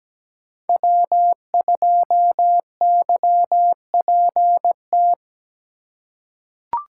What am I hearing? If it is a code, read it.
Morse code, heard as a keyed tone, about 17 wpm, 700 Hz. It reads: W2YPT
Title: USA Call Signs - 5 Characters - 4x Character Spacing - 17wpm